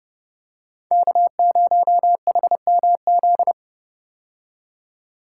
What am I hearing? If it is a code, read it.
Morse code, heard as a keyed tone, about 30 wpm, 700 Hz. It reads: K0HMZ
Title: K1USN SST QSO Practice - 30wpm